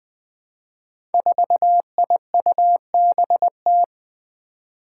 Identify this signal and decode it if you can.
Morse code, heard as a keyed tone, about 20 wpm, 700 Hz. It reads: 4IUBT
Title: Five Challenging Letter-Number Combinations 20wpm